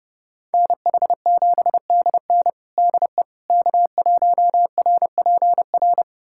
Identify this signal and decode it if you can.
Morse code, heard as a keyed tone, about 30 wpm, 700 Hz. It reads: NH7DN DE K1RPR
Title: USA Callsign DE Callsign 30wpm